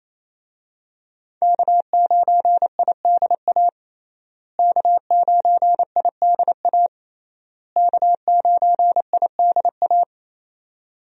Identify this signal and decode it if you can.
Morse code, heard as a keyed tone, about 28 wpm, 700 Hz. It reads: K9IDA K9IDA K9IDA
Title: USA Call Signs - 5 Characters - Repeated 3x - 28wpm